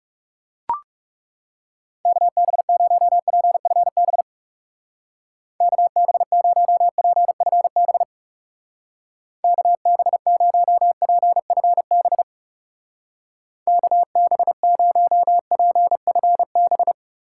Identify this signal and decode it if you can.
Morse code, heard as a keyed tone, about 38 wpm, 700 Hz. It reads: KB0PFB KB0PFB KB0PFB KB0PFB